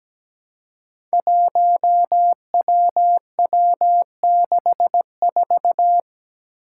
Morse code, heard as a keyed tone, about 17 wpm, 700 Hz. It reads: 1WW64
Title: Five Challenging Letter-Number Combinations 17wpm